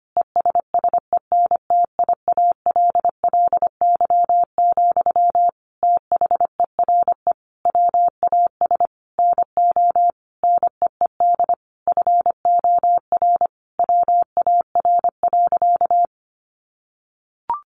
Morse code, heard as a keyed tone, about 25 wpm, 700 Hz. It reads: ESSENTIALLY, THERE WAS NO NEED FOR WAR.
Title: Sentences from Top 1300 Words 25wpm